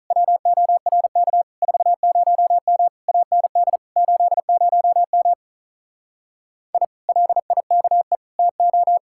Morse code, heard as a keyed tone, about 48 wpm, 700 Hz. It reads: WORK 40M AND 80M I LIKE TO